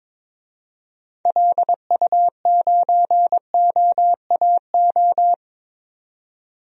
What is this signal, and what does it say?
Morse code, heard as a keyed tone, about 22 wpm, 700 Hz. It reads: LU9OAO
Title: DXCC-100 Call Signs - 6 Characters - 22wpm